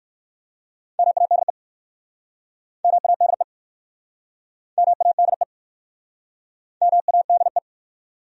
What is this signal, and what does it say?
Morse code, heard as a keyed tone, about 61 wpm, 700 Hz. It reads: MADE MADE MADE MADE